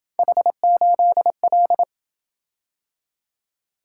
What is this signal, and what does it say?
Morse code, heard as a keyed tone, about 26 wpm, 700 Hz. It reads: H8L